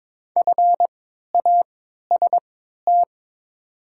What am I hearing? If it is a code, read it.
Morse code, heard as a keyed tone, about 22 wpm, spaced as for 12 wpm, 700 Hz. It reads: FAST
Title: 3x Character Spacing - Top 500 Words - 22wpm